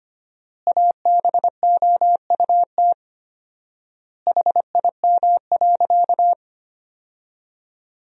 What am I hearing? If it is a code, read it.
Morse code, heard as a keyed tone, about 25 wpm, 700 Hz. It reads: ABOUT HIM.
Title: 4x Word-Spacing - Sentences from Top 200 Words - 25wpm